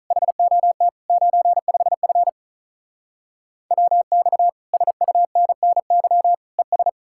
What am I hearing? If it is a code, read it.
Morse code, heard as a keyed tone, about 44 wpm, 700 Hz. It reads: HOT 95F WX SUNNY ES